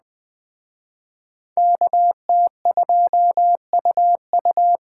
Morse code, heard as a keyed tone, about 20 wpm, 700 Hz. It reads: KT2UU
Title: Speed Racing - USA Call Signs 17wpm